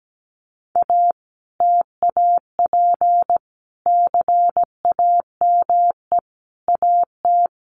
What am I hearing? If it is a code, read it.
Morse code, heard as a keyed tone, about 17 wpm, 700 Hz. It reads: A TAP CAME AT
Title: Sentences from Top 1400 Words 17wpm